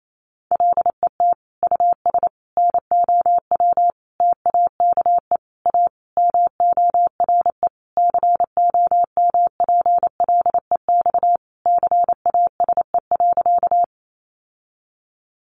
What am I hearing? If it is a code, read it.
Morse code, heard as a keyed tone, about 28 wpm, 700 Hz. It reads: LET US NOW TAKE A MORE COMPLEX CASE.
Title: Sentences from Top 900 Words 28wpm